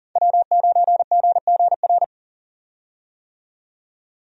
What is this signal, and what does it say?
Morse code, heard as a keyed tone, about 40 wpm, 700 Hz. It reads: W9GGR
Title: USA Call Signs - 5 Characters - 40wpm